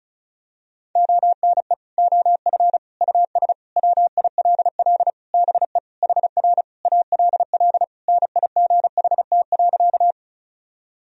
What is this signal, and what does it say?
Morse code, heard as a keyed tone, about 36 wpm, 700 Hz. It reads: ONE OF US WILL BE HR ALL NIGHT.
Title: Sentences from Top 300 Words and Abbreviations 35wpm